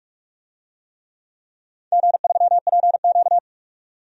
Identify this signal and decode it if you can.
Morse code, heard as a keyed tone, about 46 wpm, 700 Hz. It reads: G3PQ
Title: DXCC-100 Call Signs - 4 Characters - 45wpm